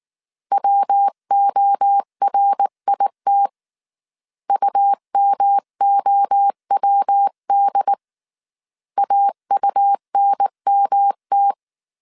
Morse code, heard as a keyed tone, about 19 wpm, 800 Hz. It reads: WORIT UMOWD AUNMT
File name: CW / DET / 1 3 1500C S B 18wpm 1916